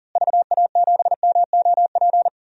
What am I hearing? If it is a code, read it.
Morse code, heard as a keyed tone, about 40 wpm, 700 Hz. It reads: VA7MOP
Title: DXCC-100 Call Signs - 6 Characters - Repeated 3x - 40wpm